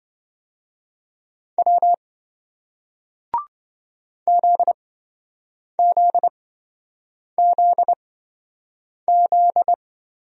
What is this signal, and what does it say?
Morse code, heard as a keyed tone, about 26 wpm, spaced as for 7 wpm, 700 Hz. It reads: W ZZZZ